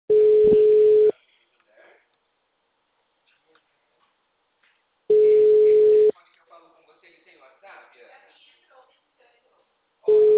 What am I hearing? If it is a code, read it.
Ringback tone (Europe, 425 Hz), with speech.